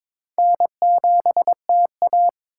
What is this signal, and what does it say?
Morse code, heard as a keyed tone, about 22 wpm, 700 Hz. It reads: N7TA
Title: USA Call Signs - 4 Characters - 6x Character Spacing - 22wpm